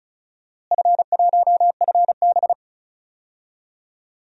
Morse code, heard as a keyed tone, about 36 wpm, 700 Hz. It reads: F1FB
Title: DXCC-100 Call Signs - 4 Characters - Repeated 3x - 35wpm